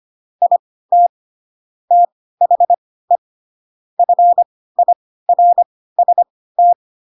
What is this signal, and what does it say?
Morse code, heard as a keyed tone, about 25 wpm, spaced as for 16 wpm, 700 Hz. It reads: IT THE FIRST
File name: Pinocchio / Pinocchio ch1 10 25 ik1hge